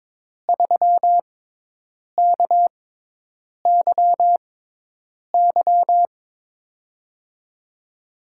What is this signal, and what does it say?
Morse code, heard as a keyed tone, about 22 wpm, spaced as for 8 wpm, 700 Hz. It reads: 3KYY
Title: USA Call Signs - 5 Characters - 6x Character Spacing - 22wpm